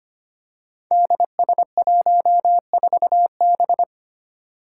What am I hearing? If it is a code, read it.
Morse code, heard as a keyed tone, about 25 wpm, 700 Hz. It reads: DS14B
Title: Five Challenging Letter-Number Combinations 25wpm